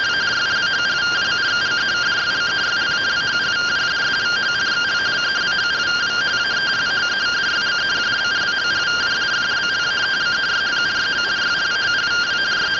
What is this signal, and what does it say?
Two-tone teleprinter signal, about 170 Hz shift.